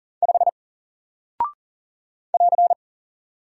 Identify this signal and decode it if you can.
Morse code, heard as a keyed tone, about 40 wpm, 700 Hz. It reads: 5 <AR>
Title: Single Character (Letter:Number:Punctuation:Prosign) Alt 40wpm